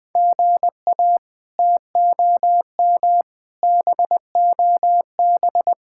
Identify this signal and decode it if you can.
Morse code, heard as a keyed tone, about 21 wpm, 700 Hz. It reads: GA TOM BOB
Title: K1USN SST QSO Practice - 20wpm